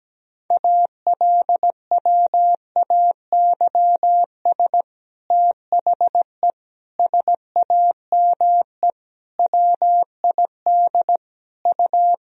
Morse code, heard as a keyed tone, about 17 wpm, 700 Hz. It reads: ALWAYS THE SAME WID U
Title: Sentences from Top 300 Words and Abbreviations 17wpm